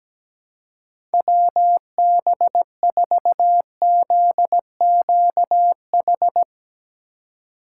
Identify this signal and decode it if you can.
Morse code, heard as a keyed tone, about 17 wpm, 700 Hz. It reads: WB4ZQH
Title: USA Call Signs - 6 Characters - 3x Character Spacing - 17wpm